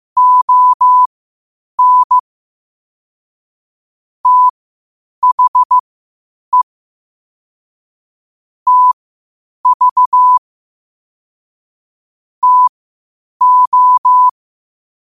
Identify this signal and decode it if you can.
Morse code, heard as a keyed tone, about 15 wpm, spaced as for 8 wpm, 1000 Hz. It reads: ON THE TV TO
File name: qotdfiles / 8wpm